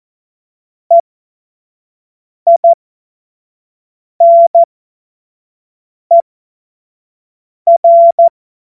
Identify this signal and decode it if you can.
Morse code, heard as a keyed tone, about 14 wpm, spaced as for 5 wpm, 680 Hz. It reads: EINER